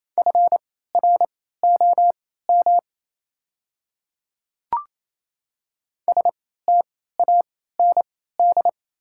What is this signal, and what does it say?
Morse code, heard as a keyed tone, about 27 wpm, spaced as for 16 wpm, 700 Hz. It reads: FROM STAND
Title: 3x Character Spacing - Top 300 Words - 28wpm